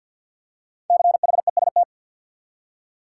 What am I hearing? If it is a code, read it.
Morse code, heard as a keyed tone, about 51 wpm, 700 Hz. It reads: CHEST
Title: Five Letter Words - 50wpm